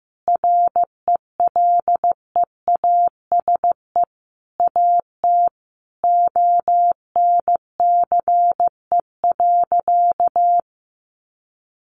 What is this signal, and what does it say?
Morse code, heard as a keyed tone, about 15 wpm, 700 Hz. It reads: RELEASE AT ONCE.
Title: Sentences from Top 1100 Words 15wpm